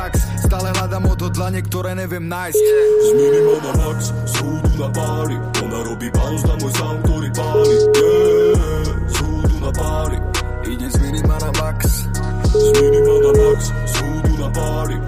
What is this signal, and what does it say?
Ringback tone (Europe, 425 Hz), with speech.